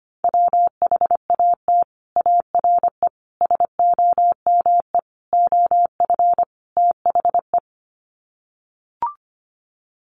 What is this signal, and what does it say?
Morse code, heard as a keyed tone, about 25 wpm, 700 Hz. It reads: WHAT ARE SOME OF THE
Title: Sets of 5 Words Encore 25wpm